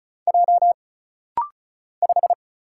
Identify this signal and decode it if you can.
Morse code, heard as a keyed tone, about 34 wpm, 700 Hz. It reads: J 5
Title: Single Letter-Number - Rapid-Fire 35wpm